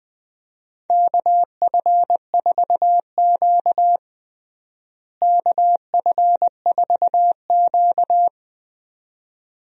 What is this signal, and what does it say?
Morse code, heard as a keyed tone, about 20 wpm, 700 Hz. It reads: KF4Q KF4Q
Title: USA Call Signs - 4 Characters - Repeated 2x - 20wpm